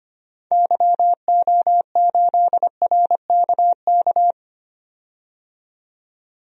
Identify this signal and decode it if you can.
Morse code, heard as a keyed tone, about 25 wpm, 700 Hz. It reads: YO8RKK
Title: DXCC-100 Call Signs - 6 Characters - 25wpm